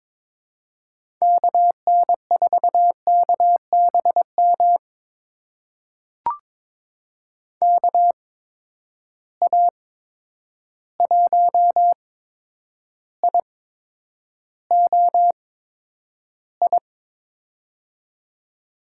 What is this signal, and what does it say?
Morse code, heard as a keyed tone, about 21 wpm, spaced as for 9 wpm, 700 Hz. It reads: KN4KBM KA1IOI
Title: USA Call Signs - 6 Characters - 8x Character Spacing - 22wpm